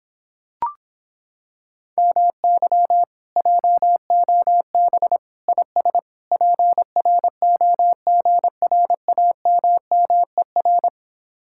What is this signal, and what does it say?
Morse code, heard as a keyed tone, about 26 wpm, 700 Hz. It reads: MY JOB IS PROGRAMMER